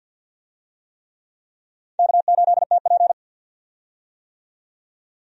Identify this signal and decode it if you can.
Morse code, heard as a keyed tone, about 52 wpm, 700 Hz. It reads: K8TP
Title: USA Call Signs - 4 Characters - 50wpm